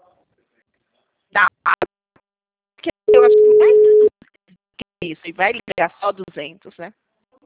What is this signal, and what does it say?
Ringback tone (Europe, 425 Hz), with speech.
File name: out- -2005- - - .23633